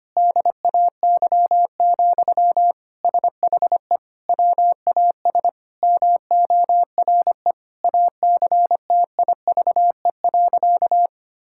Morse code, heard as a keyed tone, about 25 wpm, 700 Hz. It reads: DAY, SHE WAS MORE ACTIVE.
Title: Sentences from Top 1000 Words - Review Part 6 - 25wpm